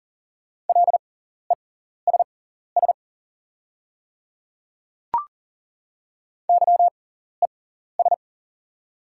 Morse code, heard as a keyed tone, about 40 wpm, spaced as for 14 wpm, 690 Hz. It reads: LESS YES
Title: 6x Character Spacing - Top 500 Words - 40wpm